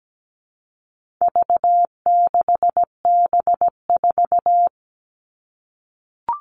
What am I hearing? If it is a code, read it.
Morse code, heard as a keyed tone, about 17 wpm, 700 Hz. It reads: V6B4
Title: All Cut Numbers - 4-Digit 17wpm